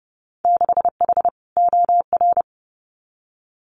Morse code, heard as a keyed tone, about 30 wpm, 700 Hz. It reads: 6H OR
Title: Winter Field Day Exchanges Encore 30wpm